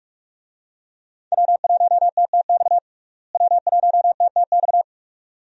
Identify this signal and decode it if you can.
Morse code, heard as a keyed tone, about 42 wpm, 700 Hz. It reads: W1TTX W1TTX
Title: USA Call Signs - 5 Characters - Repeated 2x - 45wpm